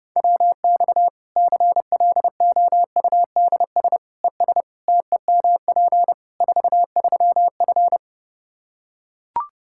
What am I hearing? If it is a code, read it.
Morse code, heard as a keyed tone, about 30 wpm, 700 Hz. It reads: WX CLOUDS ES TEMP 43F
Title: Practice QSO - Exchange 2 - Info - 30wpm